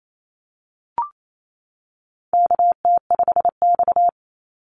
Morse code, heard as a keyed tone, about 28 wpm, 700 Hz. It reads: KT5X
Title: NA SOTA Activator Callsigns 28wpm